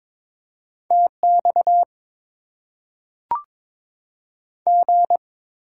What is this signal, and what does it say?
Morse code, heard as a keyed tone, about 22 wpm, 700 Hz. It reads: TX G
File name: Top 90 QSO Elements - 6x Character Spacing - 22wpm